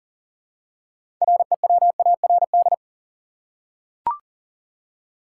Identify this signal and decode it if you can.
Morse code, heard as a keyed tone, about 40 wpm, 700 Hz. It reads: REWARD
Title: Six Letter Words - 40wpm